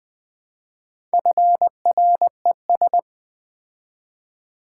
Morse code, heard as a keyed tone, about 20 wpm, 700 Hz. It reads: FRES
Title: Build A Word - 2 Letters At A Time - 8-Letter Words - 20wpm